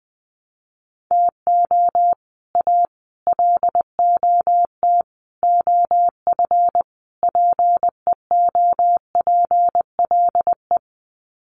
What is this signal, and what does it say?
Morse code, heard as a keyed tone, about 19 wpm, 700 Hz. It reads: TO A LOT OF PEOPLE
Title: Sets of 5 Words Encore 20wpm